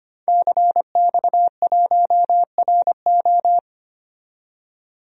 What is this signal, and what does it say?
Morse code, heard as a keyed tone, about 25 wpm, 700 Hz. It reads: CX1RO
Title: DXCC-100 Call Signs - 5 Characters - 25wpm